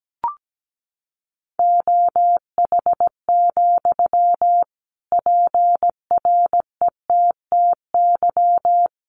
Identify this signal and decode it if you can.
Morse code, heard as a keyed tone, about 17 wpm, 700 Hz. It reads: OH, PRETTY
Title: Sentences from Top 1300 Words 17wpm